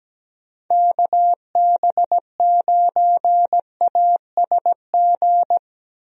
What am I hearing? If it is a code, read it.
Morse code, heard as a keyed tone, about 17 wpm, 700 Hz. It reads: KB9ASG
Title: USA Call Signs - 6 Characters - Repeated 3x - 17wpm